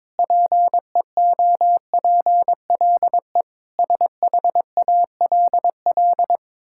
Morse code, heard as a keyed tone, about 22 wpm, 700 Hz. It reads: PEOPLE SHALL
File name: Sentences with 9-Word Phrases - 22wpm